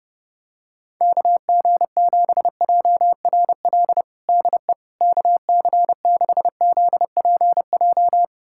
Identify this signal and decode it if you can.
Morse code, heard as a keyed tone, about 30 wpm, 700 Hz. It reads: KG7JRL DE KC6ZPJ
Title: USA Callsign DE Callsign - 6 Characters - 30wpm